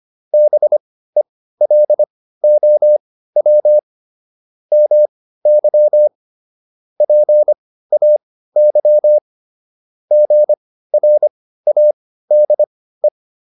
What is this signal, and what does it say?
Morse code, heard as a keyed tone, about 25 wpm, spaced as for 15 wpm, 600 Hz. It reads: BELOW MY PAY GRADE